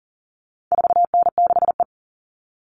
Morse code, heard as a keyed tone, about 40 wpm, 700 Hz. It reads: 4N6E
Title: All Cut Numbers - 4-Digit 40wpm